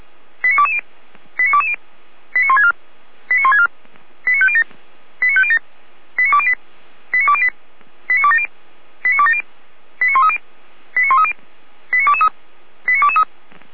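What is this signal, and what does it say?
Five-tone selective call (ZVEI), heard as quick runs of five tones, tones of about 70 ms. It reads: 89220 89220 89164 89164 89507 89507 89208 89208 89270 89270 89130 89130 89223 89223
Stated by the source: ZVEI five-tone selective call received by SDR.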